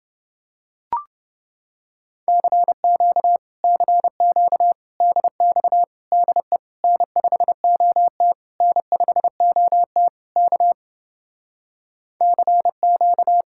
Morse code, heard as a keyed tone, about 28 wpm, 700 Hz. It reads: CQ CQ DX DE N5OT N5OT K CQ